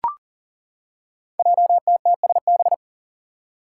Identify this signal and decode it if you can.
Morse code, heard as a keyed tone, about 42 wpm, 700 Hz. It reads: JTTSB
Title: Five Challenging Letter-Number Combinations 40wpm